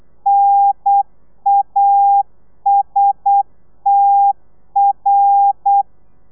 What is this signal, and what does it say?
Morse code, heard as a keyed tone, about 8 wpm, 800 Hz. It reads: NASTR
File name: telegrafia / i2tfj / Lez11 35